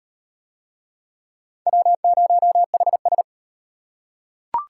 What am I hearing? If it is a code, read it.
Morse code, heard as a keyed tone, about 38 wpm, 700 Hz. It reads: W0HS